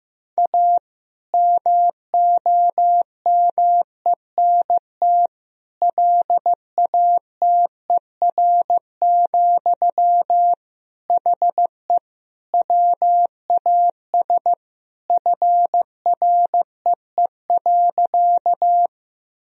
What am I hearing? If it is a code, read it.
Morse code, heard as a keyed tone, about 15 wpm, 700 Hz. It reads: A MOMENT LATER, HE WAS FREE.
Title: Sentences from Top 1000 Words - Review Part 4 - 15wpm